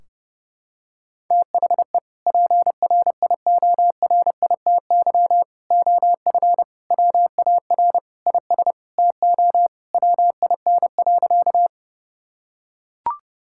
Morse code, heard as a keyed tone, about 32 wpm, 700 Hz. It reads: THE PRIORITY OF WAR IS TO WIN.
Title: Sentences from Top 1600 Words 30wpm